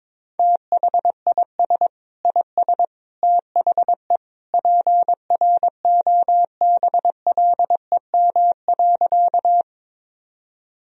Morse code, heard as a keyed tone, about 22 wpm, 700 Hz. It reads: THIS IS THE PROBLEM.
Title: Sentences from Top 1000 Words - Review Part 2 - 22wpm